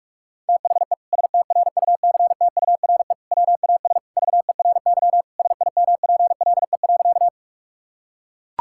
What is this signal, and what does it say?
Morse code, heard as a keyed tone, about 41 wpm, 700 Hz. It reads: THE STRUCTURE WAS VERY SIMPLE.
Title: Sentences from Top 1000 Words - Review Part 6 - 45wpm